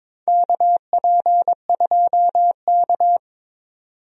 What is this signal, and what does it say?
Morse code, heard as a keyed tone, about 22 wpm, 700 Hz. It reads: KP2K
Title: USA Call Signs - 4 Characters - 22wpm